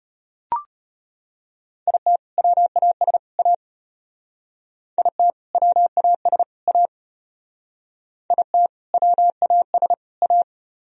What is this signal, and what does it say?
Morse code, heard as a keyed tone, about 32 wpm, 700 Hz. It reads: IT WAS A IT WAS A IT WAS A